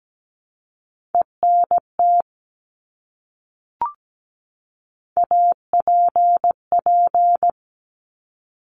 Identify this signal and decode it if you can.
Morse code, heard as a keyed tone, about 17 wpm, 700 Hz. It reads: ENT APP
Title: Top Three Letter Combinations in Words 17wpm